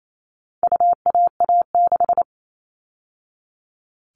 Morse code, heard as a keyed tone, about 28 wpm, 700 Hz. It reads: UAA6
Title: All Cut Numbers - 4-Digit 28wpm